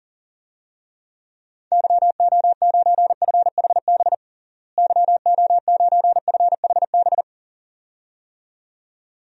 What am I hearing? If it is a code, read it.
Morse code, heard as a keyed tone, about 40 wpm, 700 Hz. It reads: YO9FHB YO9FHB
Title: DXCC-100 Call Signs - 6 Characters - Repeated 2x - 40wpm